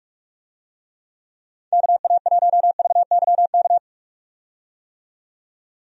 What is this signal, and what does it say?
Morse code, heard as a keyed tone, about 46 wpm, 700 Hz. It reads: KA1VYK
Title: USA Call Signs - 6 Characters - 45wpm